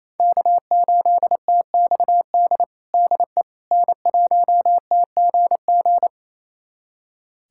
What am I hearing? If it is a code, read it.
Morse code, heard as a keyed tone, about 28 wpm, 700 Hz. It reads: K8TXD DE N1TGG
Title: USA Callsign DE Callsign - 5 Characters - 28wpm